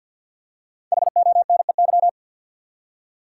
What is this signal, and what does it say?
Morse code, heard as a keyed tone, about 48 wpm, 700 Hz. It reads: HONEY
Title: Five Letter Words - 50wpm